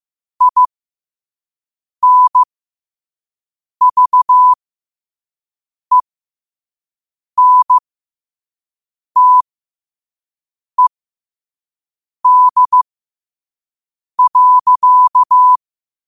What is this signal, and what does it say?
Morse code, heard as a keyed tone, about 15 wpm, spaced as for 5 wpm, 1000 Hz. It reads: INVENTED.
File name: qotdfiles / 5wpm